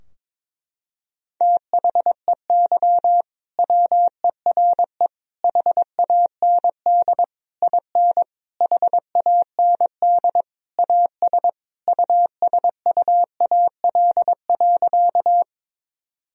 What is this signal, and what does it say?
Morse code, heard as a keyed tone, about 22 wpm, 700 Hz. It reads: THEY WERE HAND IN HAND AS USUAL.
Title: Sentences from Top 1000 Words - Review Part 2 - 22wpm